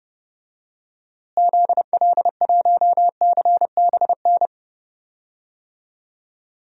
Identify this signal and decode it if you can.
Morse code, heard as a keyed tone, about 30 wpm, 700 Hz. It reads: ZL1CBN
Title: DXCC-100 Call Signs - 6 Characters - Repeated 3x - 30wpm